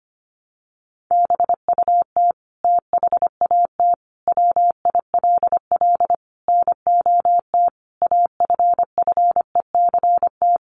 Morse code, heard as a keyed tone, about 25 wpm, 700 Hz. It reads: BUT THAT WILL NOT AFFECT
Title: Sentences from Top 900 Words 25wpm